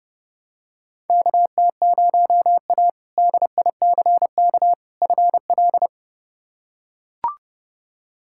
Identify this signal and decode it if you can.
Morse code, heard as a keyed tone, about 32 wpm, 700 Hz. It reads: KT0A DICK FL
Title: K1USN SST QSO Practice - 30wpm